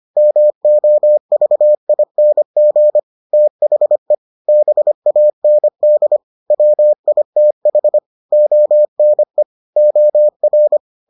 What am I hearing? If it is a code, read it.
Morse code, heard as a keyed tone, about 25 wpm, 600 Hz. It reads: MOVING THE BAND WITH ONE OR